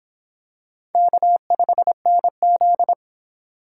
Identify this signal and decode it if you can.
Morse code, heard as a keyed tone, about 26 wpm, 700 Hz. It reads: K5NZ